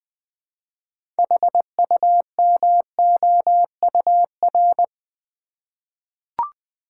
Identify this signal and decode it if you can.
Morse code, heard as a keyed tone, about 20 wpm, 700 Hz. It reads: HUMOUR
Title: 6 Letter Words - First 3 Letters Given - 20wpm